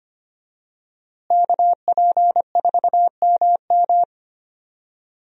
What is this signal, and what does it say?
Morse code, heard as a keyed tone, about 25 wpm, 700 Hz. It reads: KP4MM
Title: USA Call Signs - 5 Characters - Repeated 2x - 25wpm